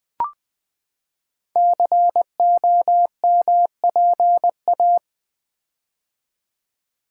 Morse code, heard as a keyed tone, about 20 wpm, 700 Hz. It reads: COMPA
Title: Build A Word - 5 Letters At A Time - 10-Letter Words - 20wpm